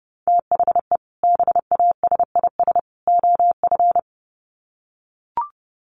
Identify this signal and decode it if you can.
Morse code, heard as a keyed tone, about 30 wpm, 700 Hz. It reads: THE BASIS OF
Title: Sets of 3 Words 30wpm